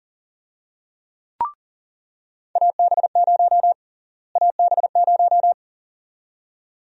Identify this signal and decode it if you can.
Morse code, heard as a keyed tone, about 40 wpm, 700 Hz. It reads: AB0 AB0
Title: USA Call Sign Prefixes - Repeated 2x - 40wpm